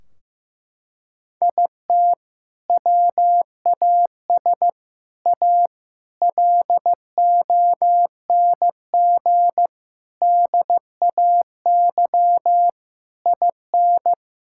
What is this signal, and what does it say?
Morse code, heard as a keyed tone, about 15 wpm, 700 Hz. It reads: IT WAS A LONG DAY IN
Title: Sentences from Top 1600 Words 15wpm